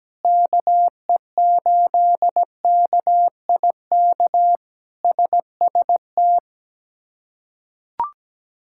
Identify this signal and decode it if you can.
Morse code, heard as a keyed tone, about 17 wpm, 700 Hz. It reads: KE8KIK SST
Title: K1USN SST QSO Practice - 17wpm